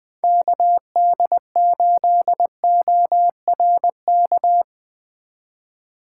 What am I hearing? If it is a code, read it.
Morse code, heard as a keyed tone, about 20 wpm, 700 Hz. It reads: KD8ORK
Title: Speed Racing - USA Call Signs 17wpm